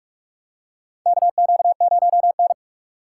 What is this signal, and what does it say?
Morse code, heard as a keyed tone, about 42 wpm, 700 Hz. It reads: KQ0N
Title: USA Call Signs - 4 Characters - 6x Character Spacing - 45wpm